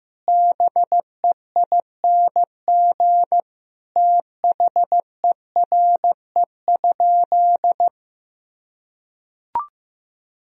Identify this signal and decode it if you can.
Morse code, heard as a keyed tone, about 16 wpm, 700 Hz. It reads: BEING THERE?
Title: Sentences from Top 1000 Words - Review Part 3 - 15wpm